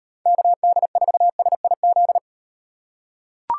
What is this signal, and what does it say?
Morse code, heard as a keyed tone, about 38 wpm, 700 Hz. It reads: KD4SIZ